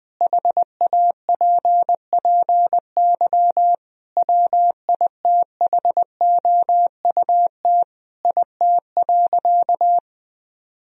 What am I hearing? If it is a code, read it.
Morse code, heard as a keyed tone, about 21 wpm, 700 Hz. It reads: HAPPY WITHOUT IT.
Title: Sentences from Top 1000 Words - Review Part 3 - 20wpm